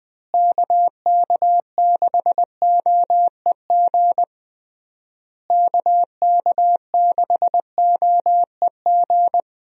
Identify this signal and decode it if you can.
Morse code, heard as a keyed tone, about 21 wpm, 700 Hz. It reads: KK6OEG KK6OEG
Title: USA Call Signs - 6 Characters - Repeated 3x - 20wpm